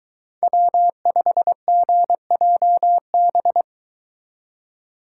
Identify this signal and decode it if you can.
Morse code, heard as a keyed tone, about 22 wpm, 700 Hz. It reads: W5GJB